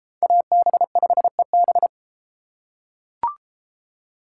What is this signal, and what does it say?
Morse code, heard as a keyed tone, about 33 wpm, 700 Hz. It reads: AB5EB